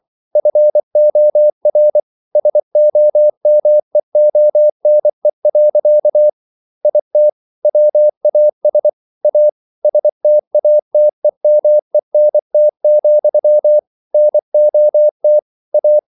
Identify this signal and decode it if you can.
Morse code, heard as a keyed tone, about 24 wpm, 600 Hz. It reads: FOR SOMEONE. IT WAS A STATEMENT, NOT A